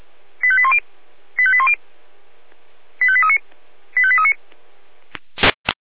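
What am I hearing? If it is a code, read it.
Five-tone selective call (ZVEI), heard as quick runs of five tones, tones of about 70 ms. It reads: 85810 85810 85839 85839
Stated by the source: ZVEI five-tone selective call 85278, received by SDR.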